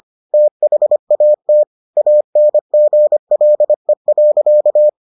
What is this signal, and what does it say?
Morse code, heard as a keyed tone, about 25 wpm, 600 Hz. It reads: THAT ANGLE.